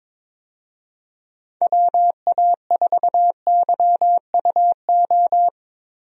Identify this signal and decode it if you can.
Morse code, heard as a keyed tone, about 22 wpm, 700 Hz. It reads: WA4YUO
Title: USA Call Signs - 6 Characters - Repeated 2x - 22wpm